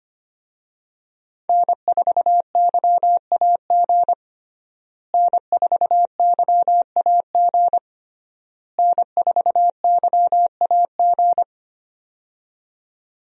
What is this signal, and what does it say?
Morse code, heard as a keyed tone, about 25 wpm, 700 Hz. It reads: N4YAG N4YAG N4YAG
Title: USA Call Signs - 5 Characters - Repeated 3x - 25wpm